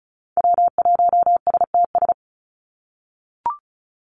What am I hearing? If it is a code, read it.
Morse code, heard as a keyed tone, about 34 wpm, 700 Hz. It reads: W1STS
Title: USA Call Signs 35wpm